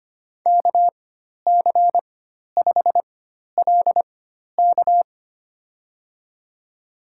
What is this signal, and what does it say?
Morse code, heard as a keyed tone, about 25 wpm, spaced as for 12 wpm, 700 Hz. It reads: KC5LK
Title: USA Call Signs - 5 Characters - 4x Character Spacing - 25wpm